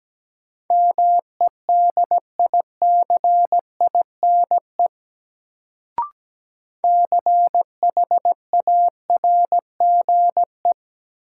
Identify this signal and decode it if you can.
Morse code, heard as a keyed tone, about 17 wpm, 700 Hz. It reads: MEDICINE CHARGE
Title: Sentences from Top 900 Words - New Word Drill - 17wpm